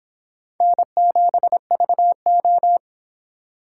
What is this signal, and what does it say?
Morse code, heard as a keyed tone, about 25 wpm, 700 Hz. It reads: N7VO